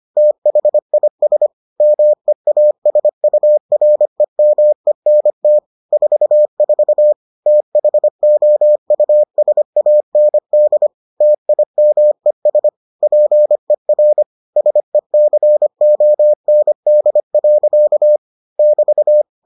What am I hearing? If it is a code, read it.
Morse code, heard as a keyed tone, about 25 wpm, 600 Hz. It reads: THIS MEASUREMENT 44 THOUSAND TIMES PER SECOND. <BT>